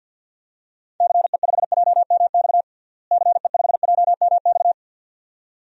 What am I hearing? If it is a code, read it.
Morse code, heard as a keyed tone, about 51 wpm, 700 Hz. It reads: CE5JMX CE5JMX
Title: DXCC-100 Call Signs - 6 Characters - Repeated 2x - 50wpm